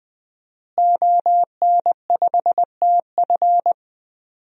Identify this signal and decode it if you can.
Morse code, heard as a keyed tone, about 20 wpm, 700 Hz. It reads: ON5TF
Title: DXCC-100 Call Signs - 5 Characters - Repeated 2x - 20wpm